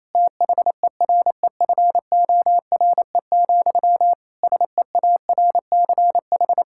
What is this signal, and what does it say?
Morse code, heard as a keyed tone, about 28 wpm, 700 Hz. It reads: THEREFORE, SEARCH
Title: Sentences from Top 1000 Words - Review Part 5 - 28wpm